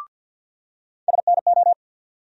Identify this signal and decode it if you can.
Morse code, heard as a keyed tone, about 50 wpm, 700 Hz. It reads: SNO
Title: USA Call Sign Postfixes - 3 Letters 50wpm